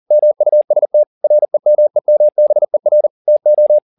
Morse code, heard as a keyed tone, about 40 wpm, 600 Hz. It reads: MUST REMEMBER TO